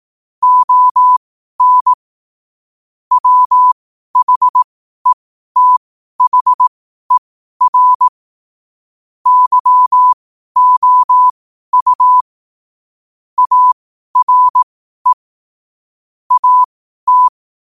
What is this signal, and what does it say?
Morse code, heard as a keyed tone, about 18 wpm, spaced as for 12 wpm, 1000 Hz. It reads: ON WHETHER YOU ARE AT
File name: qotdfiles / 12wpm